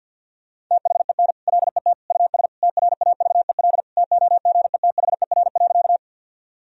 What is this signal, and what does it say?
Morse code, heard as a keyed tone, about 53 wpm, 700 Hz. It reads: THEN LET US TRAVEL TOGETHER.
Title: Sentences from Top 1000 Words - Review Part 2 - 50wpm